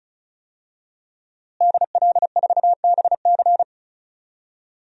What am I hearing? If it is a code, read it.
Morse code, heard as a keyed tone, about 36 wpm, 700 Hz. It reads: DL4BC
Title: DXCC-100 Call Signs - 5 Characters - Repeated 2x - 35wpm